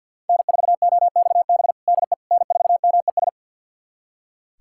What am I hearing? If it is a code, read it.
Morse code, heard as a keyed tone, about 45 wpm, 700 Hz. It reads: N4OXB DE N4MES
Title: USA Callsign DE Callsign - 5 Characters - 50wpm